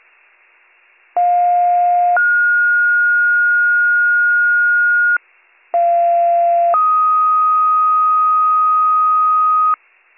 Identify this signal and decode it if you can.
Two-tone page, heard as a one-second tone, then a three-second tone, about 710 then 1430 Hz, about 690 then 1250 Hz. It is heard from 1.2 to 9.8 s.